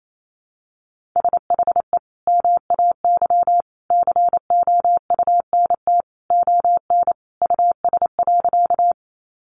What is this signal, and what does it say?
Morse code, heard as a keyed tone, about 28 wpm, 700 Hz. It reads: SHE MAY COUNT ON US.
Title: Sentences from Top 1500 Words 28wpm